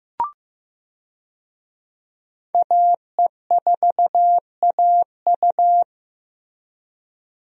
Morse code, heard as a keyed tone, about 15 wpm, 700 Hz. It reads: AE4AU
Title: USA Call Signs - 5 Characters - Repeated 2x - 15wpm